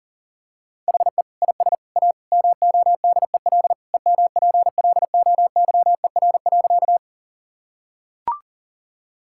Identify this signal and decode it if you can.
Morse code, heard as a keyed tone, about 37 wpm, 700 Hz. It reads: HE IS A MODEL EMPLOYER.
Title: Sentences from Top 1600 Words 40wpm